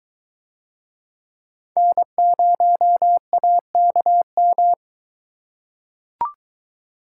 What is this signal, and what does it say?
Morse code, heard as a keyed tone, about 22 wpm, 700 Hz. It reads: N0AKM